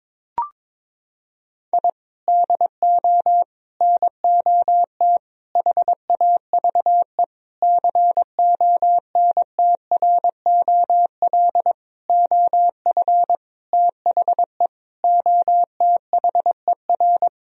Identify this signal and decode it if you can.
Morse code, heard as a keyed tone, about 22 wpm, 700 Hz. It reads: I DO NOT HAVE CONTROL OF THE OTHER
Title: Sentences from Top 1000 Words - Review Part 3 - 22wpm